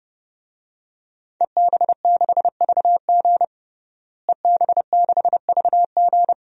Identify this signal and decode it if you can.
Morse code, heard as a keyed tone, about 30 wpm, 700 Hz. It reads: EB6VG EB6VG
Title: DXCC-100 Call Signs - 5 Characters - Repeated 2x - 30wpm